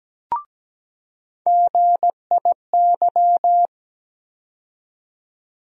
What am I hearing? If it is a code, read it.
Morse code, heard as a keyed tone, about 17 wpm, 700 Hz. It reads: GIY
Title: USA Call Sign Postfixes - 3 Letters 17wpm